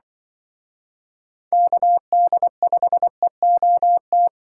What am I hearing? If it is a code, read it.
Morse code, heard as a keyed tone, about 25 wpm, 700 Hz. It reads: KD5EOT
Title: Speed Racing - USA 2x3 Call Signs 20wpm